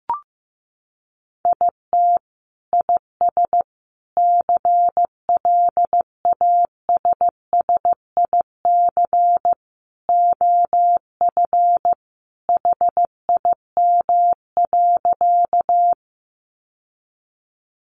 Morse code, heard as a keyed tone, about 15 wpm, 700 Hz. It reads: IT IS CLASSIC OF HIM.
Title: Sentences from Top 1000 Words 15wpm